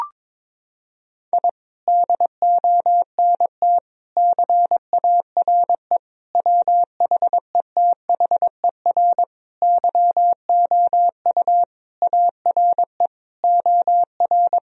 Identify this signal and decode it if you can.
Morse code, heard as a keyed tone, about 22 wpm, 700 Hz. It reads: I DONT CARE WHETHER YOU ARE OR
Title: Sentences from Top 1000 Words - Review Part 3 - 22wpm